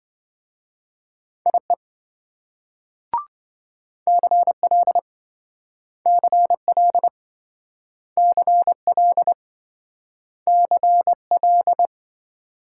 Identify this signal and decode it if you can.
Morse code, heard as a keyed tone, about 27 wpm, 700 Hz. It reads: IE CL CL CL CL